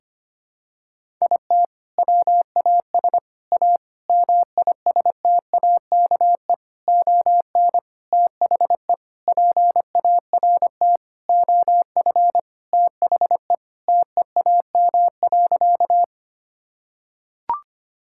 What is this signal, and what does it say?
Morse code, heard as a keyed tone, about 25 wpm, 700 Hz. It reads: IT WAS A MISTAKE ON THE PART OF THE TEAM.
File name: practice / 25wpm / Sentences with 4-Word Phrases - 25wpm